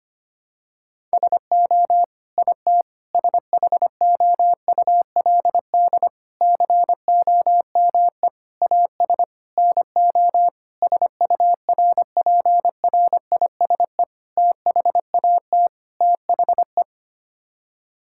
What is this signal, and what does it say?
Morse code, heard as a keyed tone, about 25 wpm, 700 Hz. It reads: SO IT SHOULD COME AS NO SURPRISE THAT THE
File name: Sets of 9 Words - 25wpm